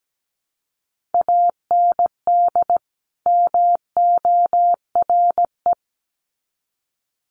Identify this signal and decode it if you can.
Morse code, heard as a keyed tone, about 17 wpm, 700 Hz. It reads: AND MORE
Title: Sets of 2 Words 17wpm